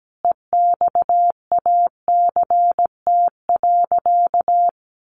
Morse code, heard as a keyed tone, about 17 wpm, 700 Hz. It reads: EXACT.
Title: Sentences from Top 1300 Words 17wpm